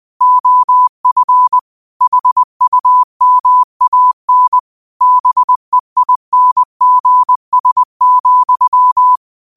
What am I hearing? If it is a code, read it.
Morse code, heard as a keyed tone, about 20 wpm, 1000 Hz. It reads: OF HUMAN BEINGS,
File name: qotdfiles / 20wpm